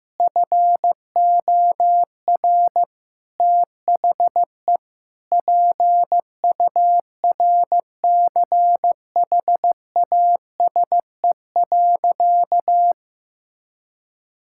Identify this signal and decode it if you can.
Morse code, heard as a keyed tone, about 16 wpm, 700 Hz. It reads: FOR THE PURCHASE.
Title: Sentences from Top 1000 Words - Review Part 5 - 15wpm